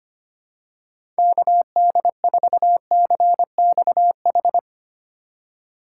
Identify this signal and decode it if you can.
Morse code, heard as a keyed tone, about 25 wpm, 700 Hz. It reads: KD4CXH
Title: USA Call Signs - 6 Characters - 4x Character Spacing - 25wpm